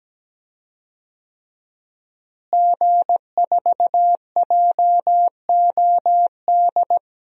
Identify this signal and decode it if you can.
Morse code, heard as a keyed tone, about 17 wpm, 700 Hz. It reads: G4JOD
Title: DXCC-100 Call Signs - 5 Characters - Repeated 3x - 17wpm